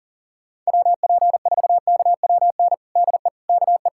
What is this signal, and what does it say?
Morse code, heard as a keyed tone, about 40 wpm, 700 Hz. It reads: WP4KWN DE KE
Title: USA Callsign DE Callsign - 6 Characters - 40wpm